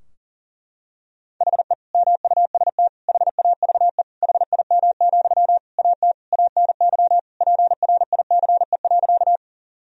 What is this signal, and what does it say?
Morse code, heard as a keyed tone, about 37 wpm, 700 Hz. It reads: HE MUST HAVE HIM, AT ANY PRICE.
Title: Sentences from Top 1000 Words - Review Part 4 - 40wpm